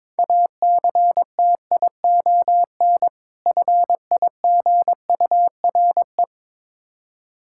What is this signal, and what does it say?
Morse code, heard as a keyed tone, about 22 wpm, 700 Hz. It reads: ACTION FIGURE
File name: practice / 22wpm / Difficult Words with Hints - 22wpm